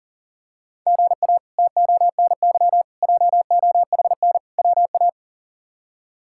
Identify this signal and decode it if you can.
Morse code, heard as a keyed tone, about 40 wpm, 690 Hz. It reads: GA TONY JOHN WA
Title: K1USN SST QSO Practice - 40wpm